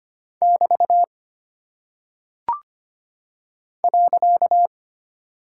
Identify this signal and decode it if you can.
Morse code, heard as a keyed tone, about 25 wpm, 700 Hz. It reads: <BT> .
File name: Single Punctuation-Prosign - 25wpm